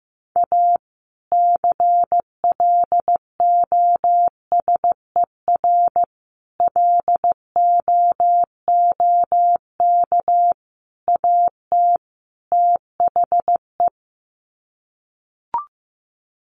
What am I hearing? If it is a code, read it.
Morse code, heard as a keyed tone, about 15 wpm, 700 Hz. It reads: A CLOSER LOOK AT THE
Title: Sets of 5 Words Encore 15wpm